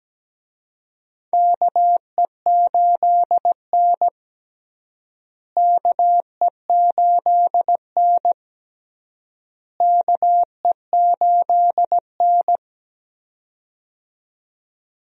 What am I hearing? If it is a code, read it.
Morse code, heard as a keyed tone, about 17 wpm, 700 Hz. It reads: KE8N KE8N KE8N
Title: USA Call Signs - 4 Characters - Repeated 3x - 17wpm